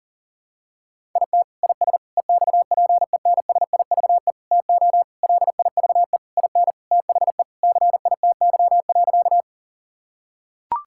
Morse code, heard as a keyed tone, about 40 wpm, 700 Hz. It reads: IT IS EXPENSIVE TO LIVE IN THE CITY.
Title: Sentences from Top 1000 Words - Review Part 6 - 40wpm